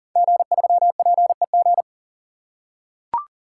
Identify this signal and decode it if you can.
Morse code, heard as a keyed tone, about 38 wpm, 700 Hz. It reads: G3PEG
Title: Code Groups - 5 Letter:Numbers - 40wpm